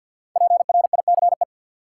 Morse code, heard as a keyed tone, about 52 wpm, 700 Hz. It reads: PRICE
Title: Sentences from Top 600 Words - New Word Drill - 50wpm